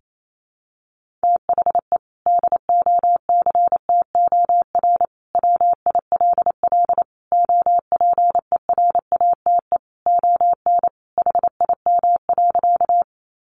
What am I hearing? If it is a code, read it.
Morse code, heard as a keyed tone, about 28 wpm, 700 Hz. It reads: THE DOCTOR WILL OPERATE ON HIM.
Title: Sentences from Top 1500 Words 28wpm